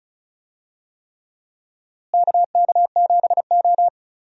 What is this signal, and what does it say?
Morse code, heard as a keyed tone, about 36 wpm, 700 Hz. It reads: KK7O
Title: USA Call Signs - 4 Characters - Repeated 3x - 35wpm